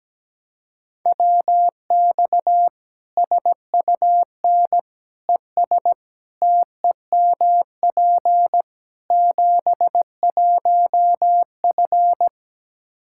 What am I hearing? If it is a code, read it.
Morse code, heard as a keyed tone, about 17 wpm, 700 Hz. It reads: WX SUN ES TEMP 71F
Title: Practice QSO - Exchange 2 - Info - 17wpm